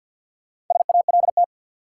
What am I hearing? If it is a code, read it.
Morse code, heard as a keyed tone, about 48 wpm, 700 Hz. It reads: SALT
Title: Sets of 5 Words - New Word Drill - 50wpm